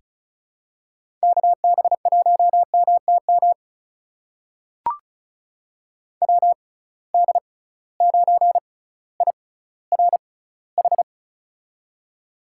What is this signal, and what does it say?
Morse code, heard as a keyed tone, about 34 wpm, spaced as for 18 wpm, 700 Hz. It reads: KB1MTM WD9IRH
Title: USA Call Signs - 6 Characters - 6x Character Spacing - 35wpm